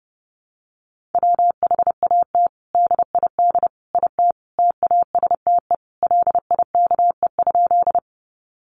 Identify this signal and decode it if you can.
Morse code, heard as a keyed tone, about 30 wpm, 700 Hz. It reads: WHAT DID IT TASTE LIKE?
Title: Sentences from Top 1100 Words 30wpm